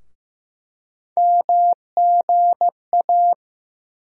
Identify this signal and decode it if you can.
Morse code, heard as a keyed tone, about 15 wpm, 700 Hz. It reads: MGA
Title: USA Call Sign Postfixes - 3 Letters 15wpm